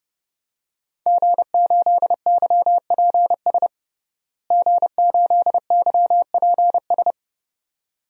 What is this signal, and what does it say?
Morse code, heard as a keyed tone, about 30 wpm, 700 Hz. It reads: G8YPS G8YPS
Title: DXCC-100 Call Signs - 5 Characters - Repeated 2x - 30wpm